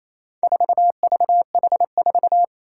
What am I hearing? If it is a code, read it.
Morse code, heard as a keyed tone, about 28 wpm, 700 Hz. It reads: 4VH4
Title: Four Challenging Letter-Number Combinations 28wpm